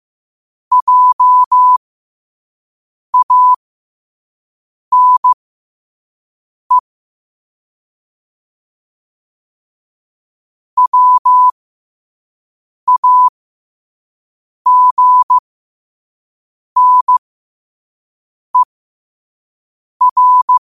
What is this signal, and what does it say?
Morse code, heard as a keyed tone, about 15 wpm, spaced as for 5 wpm, 1000 Hz. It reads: JANE WAGNER
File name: qotdfiles / 5wpm